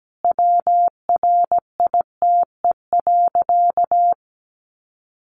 Morse code, heard as a keyed tone, about 17 wpm, 700 Hz. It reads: WRITE.
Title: Sentences from Top 800 Words 17wpm